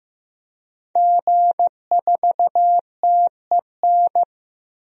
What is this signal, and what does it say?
Morse code, heard as a keyed tone, about 15 wpm, 700 Hz. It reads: G4TEN
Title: DXCC-100 Call Signs - 5 Characters - Repeated 2x - 15wpm